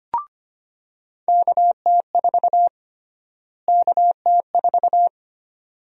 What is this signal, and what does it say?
Morse code, heard as a keyed tone, about 25 wpm, 700 Hz. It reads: KT4 KT4
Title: USA Call Sign Prefixes - Repeated 2x - 25wpm